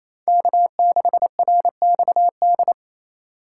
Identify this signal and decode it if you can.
Morse code, heard as a keyed tone, about 33 wpm, 700 Hz. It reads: K6RXD
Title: Speed Racing - USA 1x3 Call Signs 28wpm